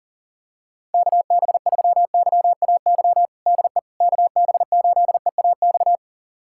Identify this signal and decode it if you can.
Morse code, heard as a keyed tone, about 37 wpm, 700 Hz. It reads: KB3YAY DE KB8EAX
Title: USA Callsign DE Callsign - 6 Characters - 40wpm